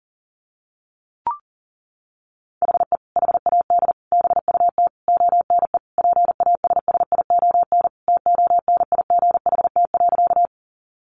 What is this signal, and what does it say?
Morse code, heard as a keyed tone, about 40 wpm, 700 Hz. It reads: HE HAD BUT ONE PASSION TONIGHT.
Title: Sentences from Top 1300 Words 40wpm